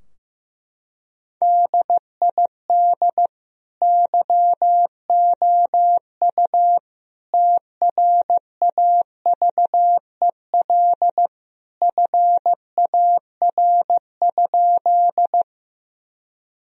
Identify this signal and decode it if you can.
Morse code, heard as a keyed tone, about 15 wpm, 700 Hz. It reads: DID YOU TRAVEL FAR?
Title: Sentences from Top 1000 Words - Review Part 2 - 15wpm